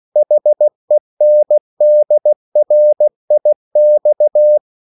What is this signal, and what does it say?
Morse code, heard as a keyed tone, about 16 wpm, 600 Hz. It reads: HENDRIX
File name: CW / Practice / EnglishQuotations0216